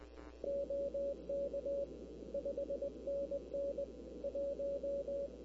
Morse code, heard as a keyed tone, about 20 wpm, 560 Hz, with noise. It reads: OK 5NN 1
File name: k5tr arrl cw dx